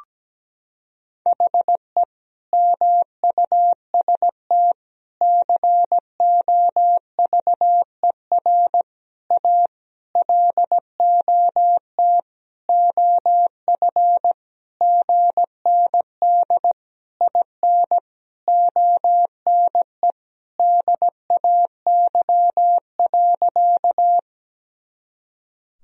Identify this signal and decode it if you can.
Morse code, heard as a keyed tone, about 17 wpm, 700 Hz. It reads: HE MUST COVER A LOT OF GND IN ONE DAY.
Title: Sentences from Top 500 Words and Abbreviations 17wpm